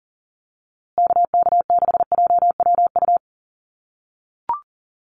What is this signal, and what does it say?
Morse code, heard as a keyed tone, about 40 wpm, 700 Hz. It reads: KK6JWU
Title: USA Call Signs Repeated 3x 40wpm